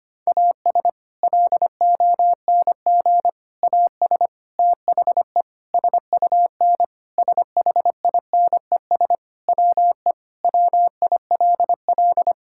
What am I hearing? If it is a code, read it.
Morse code, heard as a keyed tone, about 25 wpm, 700 Hz. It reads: AS LONG AS THE SUN SHINES WE WILL
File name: practice / 25wpm / Sentences with 4-Word Phrases - 25wpm